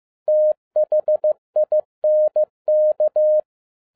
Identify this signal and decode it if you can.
Morse code, heard as a keyed tone, about 15 wpm, 610 Hz. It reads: THINK
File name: 2011-07-mp3 / n37 THE 3.11 2 15wpm